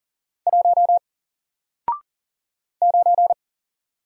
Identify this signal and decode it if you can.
Morse code, heard as a keyed tone, about 40 wpm, 700 Hz. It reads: 1 9
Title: Single Character (Letter:Number:Punctuation:Prosign) Alt 40wpm